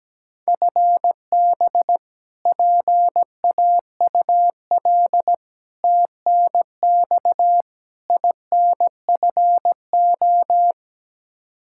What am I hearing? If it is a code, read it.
Morse code, heard as a keyed tone, about 17 wpm, 700 Hz. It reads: FB PAUL TNX INFO
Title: Practice QSO - Exchange 3 - More Info - 17wpm